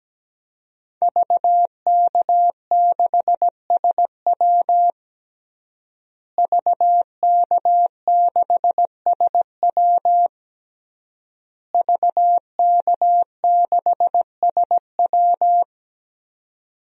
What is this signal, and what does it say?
Morse code, heard as a keyed tone, about 17 wpm, 700 Hz. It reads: VK6SW VK6SW VK6SW
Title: DXCC-100 Call Signs - 5 Characters - Repeated 3x - 17wpm